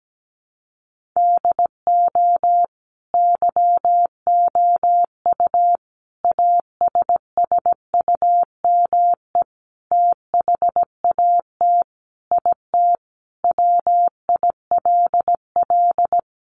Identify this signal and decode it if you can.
Morse code, heard as a keyed tone, about 17 wpm, 700 Hz. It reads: DO YOU ASSUME THAT IT WILL
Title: Sentences from Top 1500 Words 17wpm